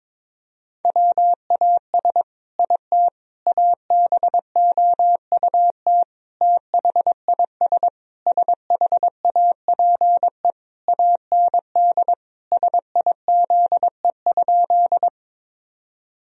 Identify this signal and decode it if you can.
Morse code, heard as a keyed tone, about 22 wpm, 700 Hz. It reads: WAS IT ABOUT THIS SHAPE AND SIZE?
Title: Sentences from Top 1000 Words - Review Part 3 - 22wpm